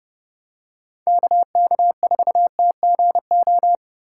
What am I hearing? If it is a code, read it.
Morse code, heard as a keyed tone, about 32 wpm, 700 Hz. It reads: KK4TGO
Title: USA Call Signs - 6 Characters - 3x Character Spacing - 30wpm